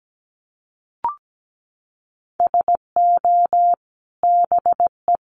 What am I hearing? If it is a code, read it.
Morse code, heard as a keyed tone, about 17 wpm, 700 Hz. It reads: SO BE
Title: A Sound Of Thunder 17wpm